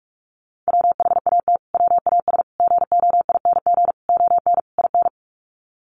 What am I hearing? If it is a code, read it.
Morse code, heard as a keyed tone, about 44 wpm, 700 Hz. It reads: WHAT WAS GOING ON IN
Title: Sets of 5 Words Encore 45wpm